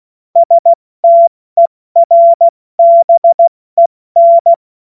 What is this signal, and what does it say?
Morse code, heard as a keyed tone, about 16 wpm, 680 Hz. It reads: STERBEN
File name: Morsekurs / Witze / Witz 15. Juni 2021, 70 F80